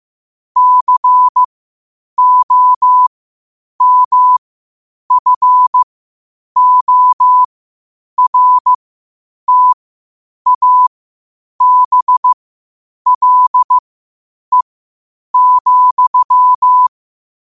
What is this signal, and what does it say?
Morse code, heard as a keyed tone, about 15 wpm, spaced as for 8 wpm, 1000 Hz. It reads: COMFORTABLE,
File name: qotdfiles / 8wpm